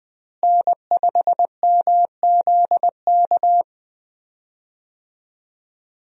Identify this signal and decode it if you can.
Morse code, heard as a keyed tone, about 20 wpm, 700 Hz. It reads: N5MZK
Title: USA Call Signs - 5 Characters - Repeated 2x - 20wpm